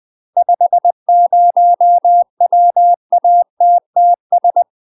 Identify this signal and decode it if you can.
Morse code, heard as a keyed tone, about 20 wpm, 700 Hz. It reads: 50 WATTS
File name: UK MASTER QSOs TRAINING / 20WPM / UK MASTER QSOs 0068